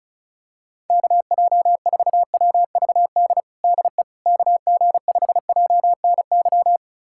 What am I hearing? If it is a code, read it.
Morse code, heard as a keyed tone, about 36 wpm, 700 Hz. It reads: KJ4WVD DE KG5JNY
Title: USA Callsign DE Callsign - 6 Characters - 35wpm